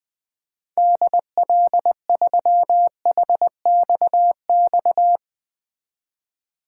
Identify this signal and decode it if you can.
Morse code, heard as a keyed tone, about 20 wpm, 700 Hz. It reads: DL3HXX
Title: 2023 SOTA Chaser Call Signs - 20wpm